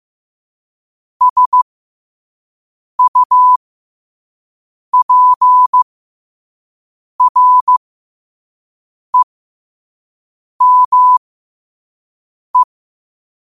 Morse code, heard as a keyed tone, about 15 wpm, spaced as for 5 wpm, 1000 Hz. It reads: SUPREME
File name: qotdfiles / 5wpm